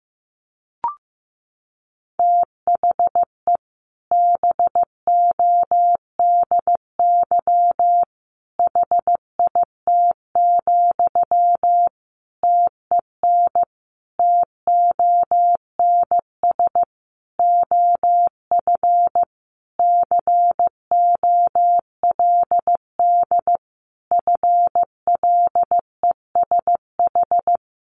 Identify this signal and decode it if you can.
Morse code, heard as a keyed tone, about 15 wpm, 700 Hz. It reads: THE BODY HIT, TEN TONS OF COLD FLESH
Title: A Sound Of Thunder 15wpm